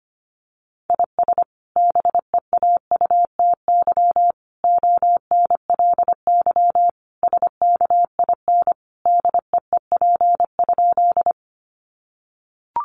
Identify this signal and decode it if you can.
Morse code, heard as a keyed tone, about 25 wpm, 700 Hz. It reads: IS BEAUTY ONLY SKIN DEEP?
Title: Sentences from Top 1300 Words 25wpm